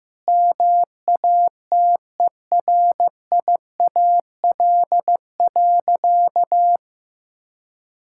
Morse code, heard as a keyed tone, about 16 wpm, 690 Hz. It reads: MATERIAL.
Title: Sentences from Top 1000 Words - Review Part 5 - 15wpm